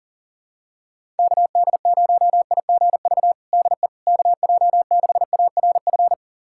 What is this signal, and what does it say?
Morse code, heard as a keyed tone, about 40 wpm, 700 Hz. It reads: KD0IGV DE KJ6ARF
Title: USA Callsign DE Callsign - 6 Characters - 40wpm